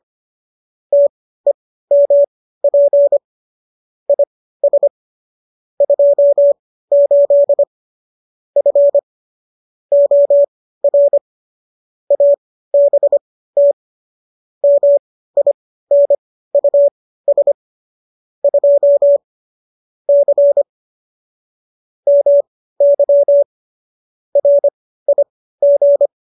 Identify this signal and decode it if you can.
Morse code, heard as a keyed tone, about 25 wpm, spaced as for 15 wpm, 570 Hz. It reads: TEMP IS 28 F OR ABT MINUS 2 C MY RIG